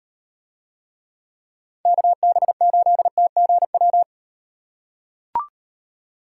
Morse code, heard as a keyed tone, about 36 wpm, 700 Hz. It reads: KB8TGW